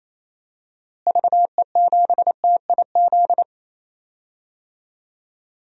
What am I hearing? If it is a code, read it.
Morse code, heard as a keyed tone, about 27 wpm, 700 Hz. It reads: VE7TIZ
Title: DXCC-100 Call Signs - 6 Characters - Repeated 2x - 28wpm